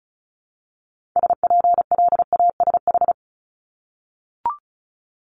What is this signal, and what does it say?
Morse code, heard as a keyed tone, about 36 wpm, 700 Hz. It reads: SPLASH
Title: Multi-letter Phonemes Repeated 3x 35wpm